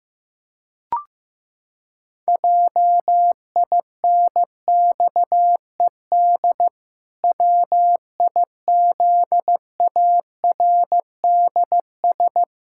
Morse code, heard as a keyed tone, about 16 wpm, 700 Hz. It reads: JINXED WIZARDS
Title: Pangram Challenge v2 - 15wpm